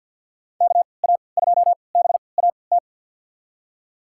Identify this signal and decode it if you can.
Morse code, heard as a keyed tone, about 51 wpm, spaced as for 28 wpm, 700 Hz. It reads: KA2BAT
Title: USA Call Signs - 6 Characters - 3x Character Spacing - 50wpm